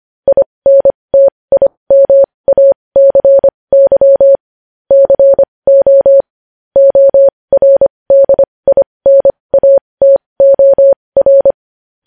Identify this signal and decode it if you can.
Morse code, heard as a keyed tone, about 25 wpm, spaced as for 20 wpm, 550 Hz. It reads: INTIMACY CO ORDINATOR
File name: podcast / News headlines-1x-25x20